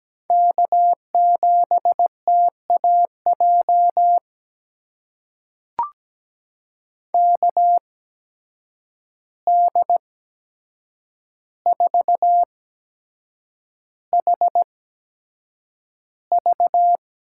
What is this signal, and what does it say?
Morse code, heard as a keyed tone, about 17 wpm, spaced as for 7 wpm, 700 Hz. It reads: K7TAJ KD4HV
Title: USA Call Signs - 5 Characters - 8x Character Spacing - 17wpm